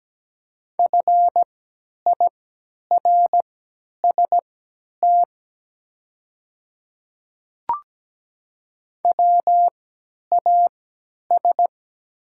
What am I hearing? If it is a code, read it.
Morse code, heard as a keyed tone, about 17 wpm, spaced as for 10 wpm, 700 Hz. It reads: FIRST WAS
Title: 3x Character Spacing - Top 100 Words - 17wpm